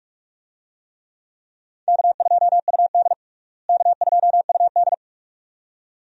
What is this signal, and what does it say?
Morse code, heard as a keyed tone, about 42 wpm, 700 Hz. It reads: K2UD K2UD
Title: USA Call Signs - 4 Characters - Repeated 2x - 45wpm